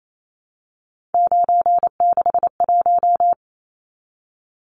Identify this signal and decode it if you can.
Morse code, heard as a keyed tone, about 28 wpm, 700 Hz. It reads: 961
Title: Numbers - 3-Digit 28wpm